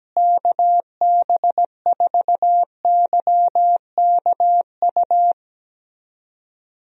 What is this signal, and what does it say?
Morse code, heard as a keyed tone, about 17 wpm, 700 Hz. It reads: KB4YKU
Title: USA Call Signs - 6 Characters - Repeated 2x - 17wpm